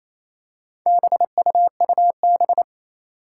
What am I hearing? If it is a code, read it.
Morse code, heard as a keyed tone, about 28 wpm, 700 Hz. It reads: BUUB
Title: Four Challenging Letter-Number Combinations 28wpm